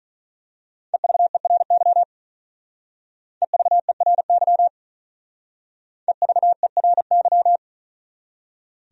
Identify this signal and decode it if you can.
Morse code, heard as a keyed tone, about 39 wpm, 700 Hz. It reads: EVERY EVERY EVERY
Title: Speed-Racing Top 300 Words 35wpm